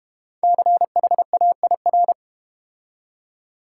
Morse code, heard as a keyed tone, about 31 wpm, 700 Hz. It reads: CHAIR